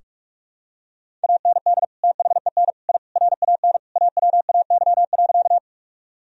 Morse code, heard as a keyed tone, about 46 wpm, 700 Hz. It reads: AND THEN I RAN AWAY.
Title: Sentences from Top 1000 Words - Review Part 3 - 45wpm